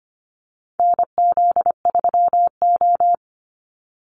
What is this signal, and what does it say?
Morse code, heard as a keyed tone, about 25 wpm, 700 Hz. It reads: NZ3O
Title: Call Signs Repeated 2x - 25wpm